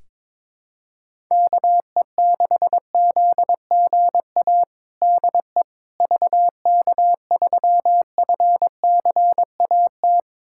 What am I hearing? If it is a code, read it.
Morse code, heard as a keyed tone, about 22 wpm, 700 Hz. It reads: KE6ZGA DE VK3FCAT
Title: USA Callsign DE Callsign 22wpm